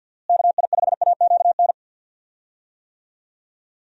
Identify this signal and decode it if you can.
Morse code, heard as a keyed tone, about 51 wpm, 700 Hz. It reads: KI5AQN
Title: USA Call Signs - 6 Characters - Repeated 2x - 50wpm